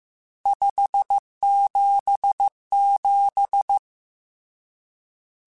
Morse code, heard as a keyed tone, about 15 wpm, 800 Hz. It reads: 577
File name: RSTs / RST-mp3s / Various RST @13